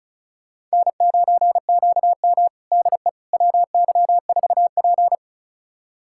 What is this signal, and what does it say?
Morse code, heard as a keyed tone, about 36 wpm, 700 Hz. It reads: N9QM DE WY4P
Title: USA Callsign DE Callsign - 4 Characters - 35wpm